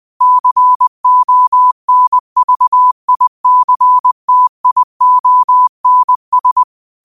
Morse code, heard as a keyed tone, about 20 wpm, 1000 Hz. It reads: CONVICTIONS